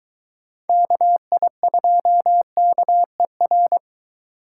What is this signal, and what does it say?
Morse code, heard as a keyed tone, about 22 wpm, 700 Hz. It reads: KI2KER